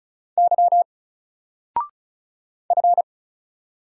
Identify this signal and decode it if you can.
Morse code, heard as a keyed tone, about 36 wpm, 700 Hz. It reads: Y F
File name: Single-Letter-Number-35wpm